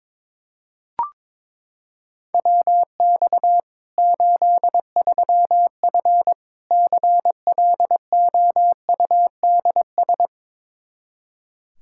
Morse code, heard as a keyed tone, about 22 wpm, 700 Hz. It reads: WX 83F CLOUDS
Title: Practice QSO - Exchange 2 - Info - 22wpm